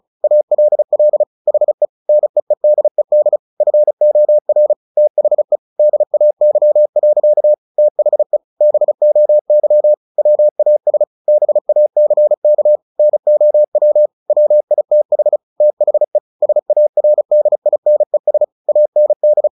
Morse code, heard as a keyed tone, about 34 wpm, 600 Hz. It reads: ALL HE NEEDED FOR THE DAY. THE BOY WAS BACK NOW WITH THE SARDINES AND
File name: the old man and the sea / the old man and the sea 1 35wpm